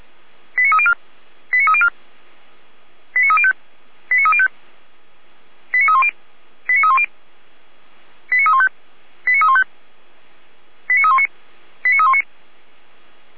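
Five-tone selective call (ZVEI), heard as quick runs of five tones, tones of about 70 ms. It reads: 89304 89304 89305 89305 89310 89310 89315 89315 89319 89319
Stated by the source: ZVEI five-tone call 89276 received by SDR, from the feuerwehr-orlamuende.org fire brigade audio archive.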